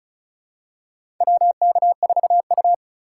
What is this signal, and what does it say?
Morse code, heard as a keyed tone, about 36 wpm, 700 Hz. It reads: WK4U
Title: USA Call Signs - 4 Characters - 3x Character Spacing - 35wpm